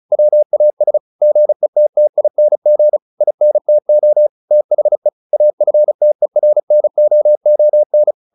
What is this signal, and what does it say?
Morse code, heard as a keyed tone, about 34 wpm, 600 Hz. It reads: WAS GETTING INTO THE AFTERNOON
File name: the old man and the sea / the old man and the sea 1 35wpm